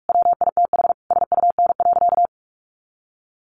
Morse code, heard as a keyed tone, about 44 wpm, 700 Hz. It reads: WITH SUN.
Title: A Sound Of Thunder 45wpm